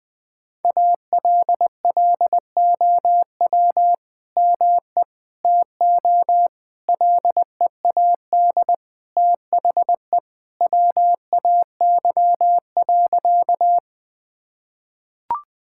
Morse code, heard as a keyed tone, about 20 wpm, 700 Hz. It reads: ALLOW ME TO LEAD THE WAY.
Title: Sentences from Top 1000 Words - Review Part 5 - 20wpm